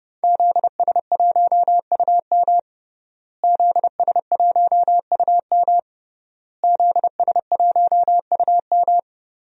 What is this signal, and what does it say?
Morse code, heard as a keyed tone, about 30 wpm, 700 Hz. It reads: ZS1UM ZS1UM ZS1UM
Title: DXCC-100 Call Signs - 5 Characters - Repeated 3x - 30wpm